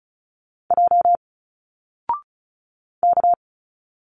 Morse code, heard as a keyed tone, about 36 wpm, 700 Hz. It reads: J K
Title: Single Letters 35wpm